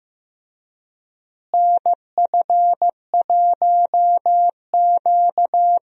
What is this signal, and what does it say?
Morse code, heard as a keyed tone, about 15 wpm, 700 Hz. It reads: NF1Q
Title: USA Call Signs - 4 Characters - Repeated 3x - 15wpm